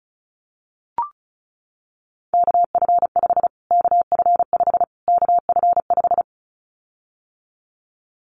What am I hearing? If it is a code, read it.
Morse code, heard as a keyed tone, about 34 wpm, 700 Hz. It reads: KF5 KF5 KF5
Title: Callsign Prefixes for Top 25 DXCC - Repeated 3x - 35wpm